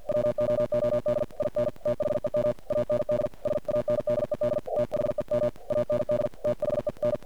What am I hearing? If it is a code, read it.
Morse code, heard as a keyed tone, about 48 wpm, 590 Hz, with noise. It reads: WOOD IN THEM AND SANDED THEM AND THEN
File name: archived / CRSnet